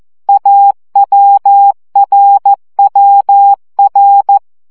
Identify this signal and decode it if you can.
Morse code, heard as a keyed tone, about 14 wpm, 800 Hz. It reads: AWRWR
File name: CW gyakorlas a r w 12 baud